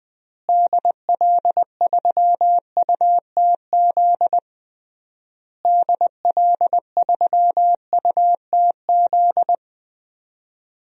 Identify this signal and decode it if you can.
Morse code, heard as a keyed tone, about 20 wpm, 700 Hz. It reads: DL3UTZ DL3UTZ
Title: DXCC-100 Call Signs - 6 Characters - Repeated 3x - 20wpm